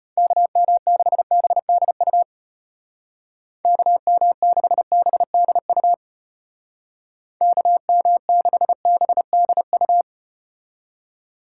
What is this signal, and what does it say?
Morse code, heard as a keyed tone, about 34 wpm, 700 Hz. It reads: KM6BDU KM6BDU KM6BDU